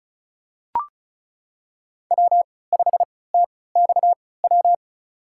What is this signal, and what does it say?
Morse code, heard as a keyed tone, about 34 wpm, spaced as for 20 wpm, 700 Hz. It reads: W5TXW
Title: USA Call Signs - 5 Characters - 3x Character Spacing - 35wpm